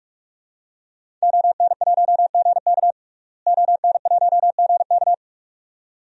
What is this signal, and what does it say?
Morse code, heard as a keyed tone, about 41 wpm, 700 Hz. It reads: ON1GK ON1GK
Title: DXCC-100 Call Signs - 5 Characters - Repeated 2x - 45wpm